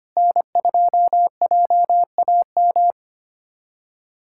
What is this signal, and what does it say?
Morse code, heard as a keyed tone, about 25 wpm, 700 Hz. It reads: N2JAM
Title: USA Callsign DE Callsign - 5 Characters - 25wpm